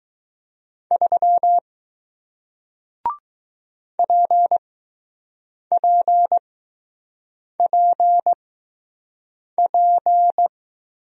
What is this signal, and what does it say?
Morse code, heard as a keyed tone, about 20 wpm, spaced as for 6 wpm, 700 Hz. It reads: 3 PPPP